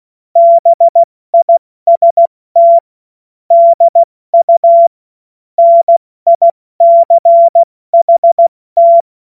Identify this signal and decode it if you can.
Morse code, heard as a keyed tone, about 16 wpm, 680 Hz. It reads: BIST DU NICHT
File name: Morsekurs / Witze / Witz 24. Januar 2023 70 F80